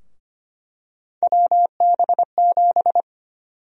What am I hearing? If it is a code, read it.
Morse code, heard as a keyed tone, about 25 wpm, 700 Hz. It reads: WB7
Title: USA Call Sign Prefixes - Repeated 2x - 25wpm